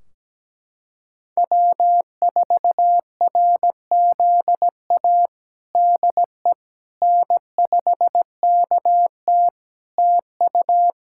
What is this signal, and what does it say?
Morse code, heard as a keyed tone, about 17 wpm, 700 Hz. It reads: W4RZA DE N5KT TU
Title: Practice QSO - Exchange 4 - Ending - 17wpm